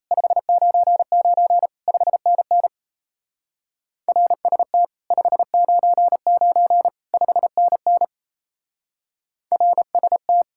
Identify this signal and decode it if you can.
Morse code, heard as a keyed tone, about 36 wpm, 700 Hz. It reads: 599 5NN RST 599 5NN RST